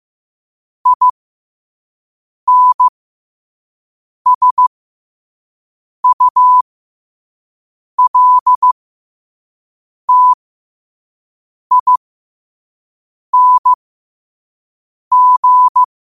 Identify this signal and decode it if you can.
Morse code, heard as a keyed tone, about 15 wpm, spaced as for 5 wpm, 1000 Hz. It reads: INSULTING
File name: qotdfiles / 5wpm